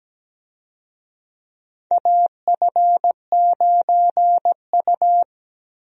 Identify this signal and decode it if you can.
Morse code, heard as a keyed tone, about 17 wpm, 700 Hz. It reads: AF9U
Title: USA Call Signs - 4 Characters - Repeated 2x - 17wpm